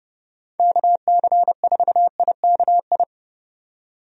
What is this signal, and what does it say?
Morse code, heard as a keyed tone, about 30 wpm, 700 Hz. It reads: KC4IKI
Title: USA Call Signs - 6 Characters - 30wpm